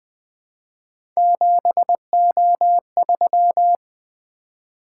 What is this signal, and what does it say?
Morse code, heard as a keyed tone, about 21 wpm, 700 Hz. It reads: 7O3
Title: Three Letter-Number Combinations - 20wpm